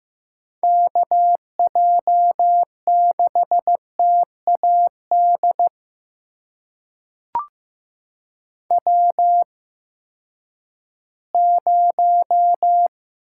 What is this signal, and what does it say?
Morse code, heard as a keyed tone, about 15 wpm, spaced as for 10 wpm, 700 Hz. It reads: KJ6TAD W0
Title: USA Call Signs - 8x Character Spacing - 15wpm